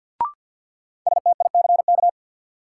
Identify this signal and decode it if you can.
Morse code, heard as a keyed tone, about 53 wpm, 700 Hz. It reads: STICK
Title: Sentences from Top 1200 Words - New Word Drill Alt - 50wpm